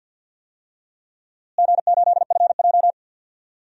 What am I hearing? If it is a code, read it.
Morse code, heard as a keyed tone, about 51 wpm, 700 Hz. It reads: G8FJ
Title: DXCC-100 Call Signs - 4 Characters - Repeated 3x - 50wpm